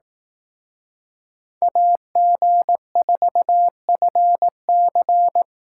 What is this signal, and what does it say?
Morse code, heard as a keyed tone, about 18 wpm, 700 Hz. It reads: AG4FC
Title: Speed Racing - USA 2x2 Call Signs 15wpm